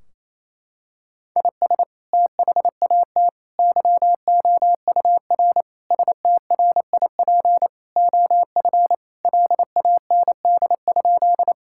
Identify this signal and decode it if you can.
Morse code, heard as a keyed tone, about 28 wpm, 700 Hz. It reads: IS THAT YOUR STRIP OF LAND?
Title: Sentences from Top 1600 Words 28wpm